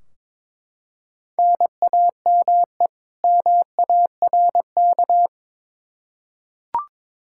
Morse code, heard as a keyed tone, about 22 wpm, 700 Hz. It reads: NAME MARK
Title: Practice QSO - Exchange 1 - Beginning - 22wpm